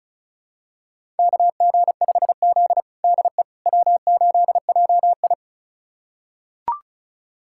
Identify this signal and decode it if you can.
Morse code, heard as a keyed tone, about 36 wpm, 700 Hz. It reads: KG5Z DE W8JI
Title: USA Callsign DE Callsign - 4 Characters - 35wpm